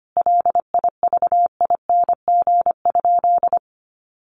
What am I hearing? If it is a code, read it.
Morse code, heard as a keyed tone, about 25 wpm, 700 Hz. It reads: LIVING?
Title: Sentences from Top 1100 Words 25wpm